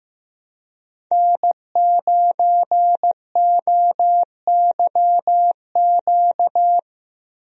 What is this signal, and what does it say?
Morse code, heard as a keyed tone, about 15 wpm, 700 Hz. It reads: N9OYQ
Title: USA Call Signs - 5 Characters - Repeated 3x - 15wpm